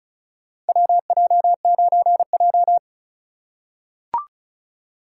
Morse code, heard as a keyed tone, about 36 wpm, 700 Hz. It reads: WJ9J
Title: USA Call Signs - 4 Characters - 3x Character Spacing - 35wpm